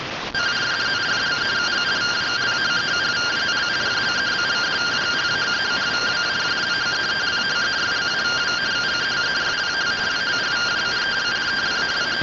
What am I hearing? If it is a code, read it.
Two-tone teleprinter signal, about 170 Hz shift.